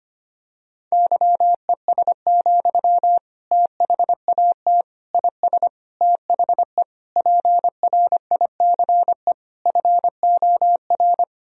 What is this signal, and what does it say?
Morse code, heard as a keyed tone, about 25 wpm, 700 Hz. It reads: YES, THAT IS THE PRICE FOR
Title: Sentences from Top 1000 Words - Review Part 4 - 25wpm